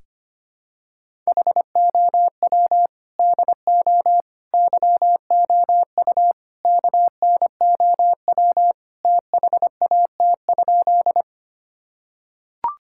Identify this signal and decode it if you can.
Morse code, heard as a keyed tone, about 25 wpm, 700 Hz. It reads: HOW DO YOU KNOW THAT?
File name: practice / 25wpm / Sentences with 4-Word Phrases - 25wpm